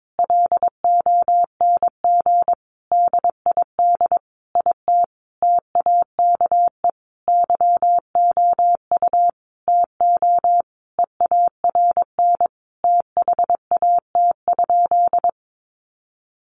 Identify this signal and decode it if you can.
Morse code, heard as a keyed tone, about 22 wpm, 700 Hz. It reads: LONG DID IT TAKE YOU TO EARN THAT?
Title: Sentences from Top 1500 Words 22wpm